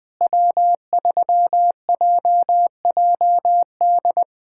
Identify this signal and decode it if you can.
Morse code, heard as a keyed tone, about 21 wpm, 700 Hz. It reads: W3JJD
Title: USA Callsign DE Callsign - 5 Characters - 20wpm